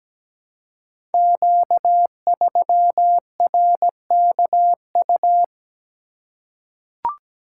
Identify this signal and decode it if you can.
Morse code, heard as a keyed tone, about 17 wpm, 700 Hz. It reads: Q3RKU
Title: Code Groups - 5 Letter:Numbers - 17wpm